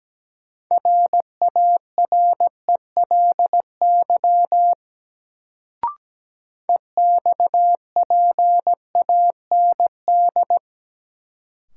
Morse code, heard as a keyed tone, about 17 wpm, 700 Hz. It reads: RARELY EXPAND
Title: Sentences from Top 1400 Words - New Word Drill - 17wpm